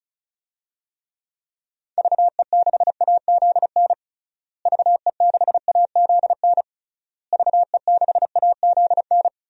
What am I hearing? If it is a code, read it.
Morse code, heard as a keyed tone, about 36 wpm, 700 Hz. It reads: VE6AZN VE6AZN VE6AZN
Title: DXCC-100 Call Signs - 6 Characters - Repeated 3x - 35wpm